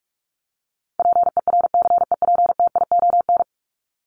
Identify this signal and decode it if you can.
Morse code, heard as a keyed tone, about 44 wpm, 700 Hz. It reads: PERCEPTION
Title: Top 1751-2000 Words 45wpm